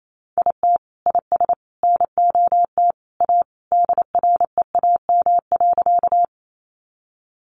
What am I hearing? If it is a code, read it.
Morse code, heard as a keyed tone, about 28 wpm, 700 Hz. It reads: IT IS NOT A DREAM.
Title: Sentences from Top 1500 Words 28wpm